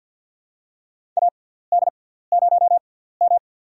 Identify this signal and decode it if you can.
Morse code, heard as a keyed tone, about 53 wpm, spaced as for 17 wpm, 700 Hz. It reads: AD0M
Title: USA Call Signs - 4 Characters - 6x Character Spacing - 50wpm